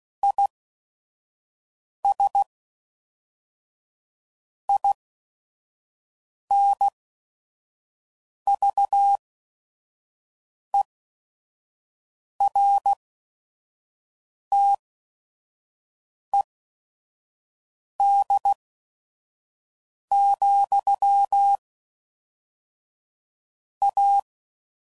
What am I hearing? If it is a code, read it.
Morse code, heard as a keyed tone, about 16 wpm, spaced as for 5 wpm, 800 Hz. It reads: IS INVERTED, A
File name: CW Code Groups @5WPM / Technical Subjects / XC